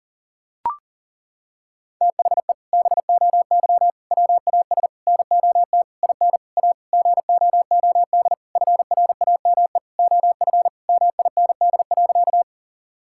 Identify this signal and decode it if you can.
Morse code, heard as a keyed tone, about 40 wpm, 700 Hz. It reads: THE BOY WAS NOT IN A GOOD FRAME OF MIND.
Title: Sentences from Top 1000 Words - Review Part 6 - 40wpm